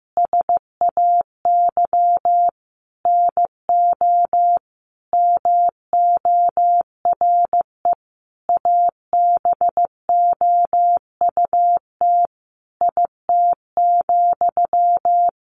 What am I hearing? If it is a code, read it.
Morse code, heard as a keyed tone, about 15 wpm, 700 Hz. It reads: SAY NO MORE ABOUT IT,
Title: Sentences from Top 200 Words 15wpm